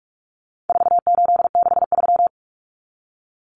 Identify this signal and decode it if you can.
Morse code, heard as a keyed tone, about 44 wpm, 700 Hz. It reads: 4863
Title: Common Cut Numbers - 4-Digit 45wpm